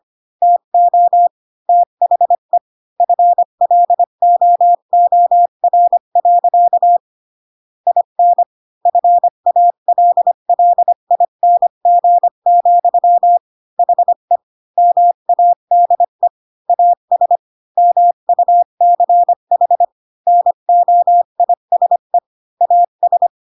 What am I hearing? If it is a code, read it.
Morse code, heard as a keyed tone, about 25 wpm, 700 Hz. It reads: TO THE FLOOR. IN FALLING, HE MADE AS MUCH NOISE AS
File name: Pinocchio / Pinocchio ch7 20 25 ik1hge